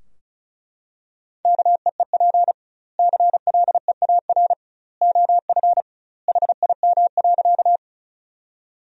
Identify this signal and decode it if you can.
Morse code, heard as a keyed tone, about 36 wpm, 700 Hz. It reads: KEEP CLEAR OF HIM.
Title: 2x Word-Spacing - Sentences from Top 500 Words - 35wpm